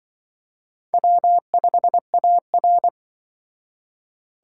Morse code, heard as a keyed tone, about 24 wpm, 700 Hz. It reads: W5AR
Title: Speed Racing - USA 1x2 Call Signs 20wpm